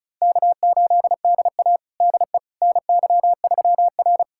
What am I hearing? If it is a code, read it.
Morse code, heard as a keyed tone, about 36 wpm, 700 Hz. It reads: K8DA DE NY3R
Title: USA Callsign DE Callsign - 4 Characters - 35wpm